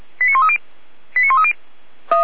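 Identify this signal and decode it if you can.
Five-tone selective call (ZVEI), heard as quick runs of five tones, tones of about 80 ms. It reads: 89130 89130
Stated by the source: ZVEI five-tone selective call received on an SDR.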